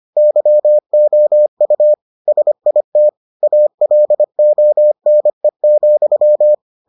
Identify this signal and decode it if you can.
Morse code, heard as a keyed tone, about 25 wpm, 600 Hz. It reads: YOU SIT ALONE,